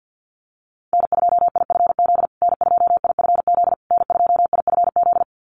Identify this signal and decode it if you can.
Morse code, heard as a keyed tone, about 48 wpm, 700 Hz. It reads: N2IFZ N2IFZ N2IFZ
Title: USA Call Signs Repeated 3x 50wpm